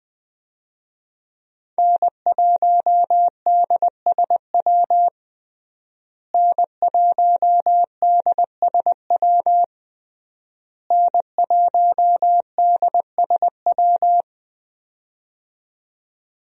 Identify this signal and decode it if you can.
Morse code, heard as a keyed tone, about 20 wpm, 700 Hz. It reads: N1DSW N1DSW N1DSW
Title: USA Call Signs - 5 Characters - Repeated 3x - 20wpm